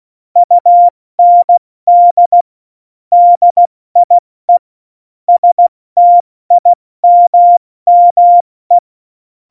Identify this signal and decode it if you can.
Morse code, heard as a keyed tone, about 16 wpm, spaced as for 14 wpm, 700 Hz. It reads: UND DIE STIMME
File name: Morsekurs / Witze / Witz 02. Oktober 2018, 70 F80